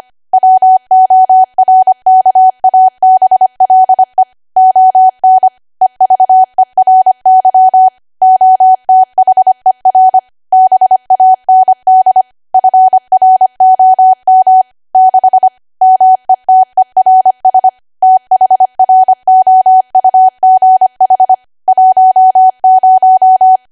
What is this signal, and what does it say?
Morse code, heard as a keyed tone, about 25 wpm, 750 Hz. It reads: WORKABLE ON EVERY OTHER BAND FROM 6 METERS THROUGH 10
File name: Archive / 25 WPM / 25WPM